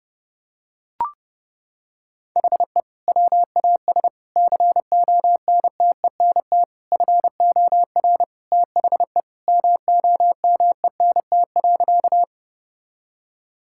Morse code, heard as a keyed tone, about 30 wpm, 700 Hz. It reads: HE WAS CONTENT FOR THE MOMENT.
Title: Sentences from Top 1000 Words - Review Part 6 - 30wpm